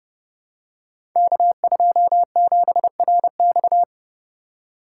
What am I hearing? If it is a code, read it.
Morse code, heard as a keyed tone, about 30 wpm, 700 Hz. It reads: K27RX
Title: Code Groups - 5 Letter:Numbers - 30wpm